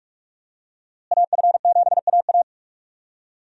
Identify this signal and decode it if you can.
Morse code, heard as a keyed tone, about 46 wpm, 700 Hz. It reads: AF7AA
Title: USA Call Signs - 5 Characters - Repeated 3x - 45wpm